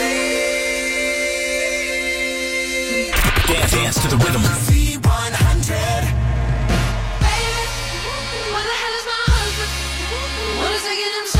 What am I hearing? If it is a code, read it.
Dial-up modem handshake.